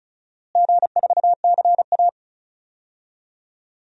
Morse code, heard as a keyed tone, about 36 wpm, 700 Hz. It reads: G4CA
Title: DXCC-100 Call Signs - 4 Characters - Repeated 2x - 35wpm